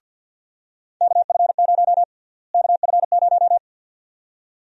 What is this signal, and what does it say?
Morse code, heard as a keyed tone, about 51 wpm, 700 Hz. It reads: KF0 KF0
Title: USA Call Sign Prefixes - Repeated 2x - 50wpm